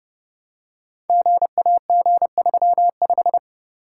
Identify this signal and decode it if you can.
Morse code, heard as a keyed tone, about 30 wpm, 700 Hz. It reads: GAG35
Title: Code Groups - 5 Letter:Numbers - 30wpm